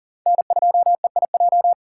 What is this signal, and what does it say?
Morse code, heard as a keyed tone, about 40 wpm, 700 Hz. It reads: N2EIJ
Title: USA Call Signs - 5 Characters - 2x Character Spacing - 40wpm